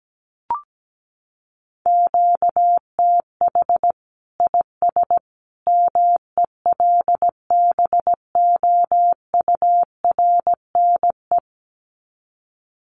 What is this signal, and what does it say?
Morse code, heard as a keyed tone, about 17 wpm, 700 Hz. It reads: QTH IS MELBOURNE
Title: Common QSO Phrases 17wpm